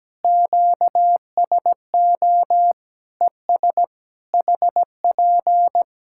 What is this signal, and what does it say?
Morse code, heard as a keyed tone, about 17 wpm, 700 Hz. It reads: QSO ES HP
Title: Practice QSO - Exchange 4 - Ending - 17wpm